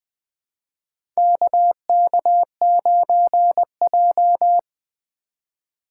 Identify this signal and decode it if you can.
Morse code, heard as a keyed tone, about 21 wpm, 700 Hz. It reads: KK9J
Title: USA Call Signs - 4 Characters - 2x Character Spacing - 20wpm